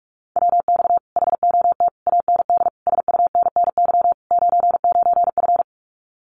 Morse code, heard as a keyed tone, about 46 wpm, 700 Hz. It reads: WX HOT AND SUNNY 99F
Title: Common QSO Phrases Encore 45wpm